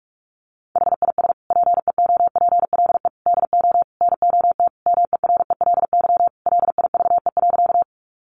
Morse code, heard as a keyed tone, about 44 wpm, 700 Hz. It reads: HIS PEOPLE DO NOT MERELY LIVE.
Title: Sentences from Top 1400 Words 45wpm